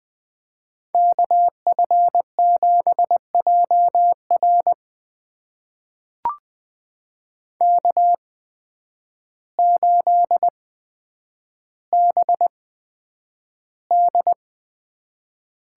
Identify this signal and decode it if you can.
Morse code, heard as a keyed tone, about 20 wpm, spaced as for 9 wpm, 700 Hz. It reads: KF7JR K8BD
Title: USA Call Signs - 5 Characters - 8x Character Spacing - 20wpm